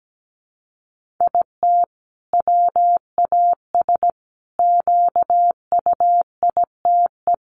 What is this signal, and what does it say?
Morse code, heard as a keyed tone, about 17 wpm, 700 Hz. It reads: IT WAS QUITE
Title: Sentences from Top 1500 Words 17wpm